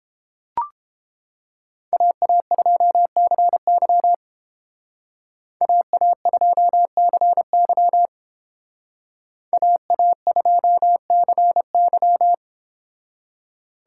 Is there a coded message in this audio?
Yes: Morse code from 1.9 to 12.3 s, heard as a keyed tone, about 31 wpm, 700 Hz. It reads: AA2CY AA2CY AA2CY